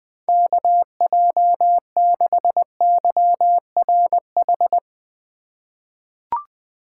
Morse code, heard as a keyed tone, about 21 wpm, 700 Hz. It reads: KJ6YRH
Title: USA Call Signs - 6 Characters - 2x Character Spacing - 20wpm